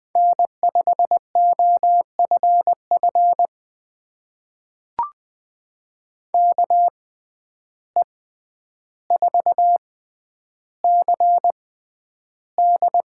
Morse code, heard as a keyed tone, about 20 wpm, spaced as for 10 wpm, 700 Hz. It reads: N5OFF KE4CD
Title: USA Call Signs - 5 Characters - 6x Character Spacing - 20wpm